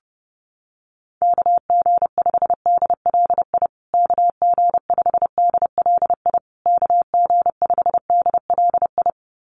Morse code, heard as a keyed tone, about 28 wpm, 700 Hz. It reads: KG5DLI KG5DLI KG5DLI
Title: USA Call Signs Repeated 3x 30wpm